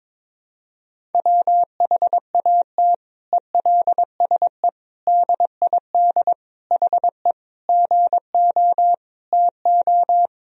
Morse code, heard as a keyed tone, about 22 wpm, 700 Hz. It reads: WHAT ELSE DID HE GO TO
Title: Sentences from Top 1000 Words - Review Part 1 - 22wpm